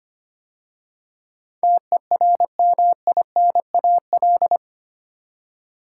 Morse code, heard as a keyed tone, about 25 wpm, 690 Hz. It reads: TERMINAL
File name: 8 Letter Words - First 3 Letters Given - 25wpm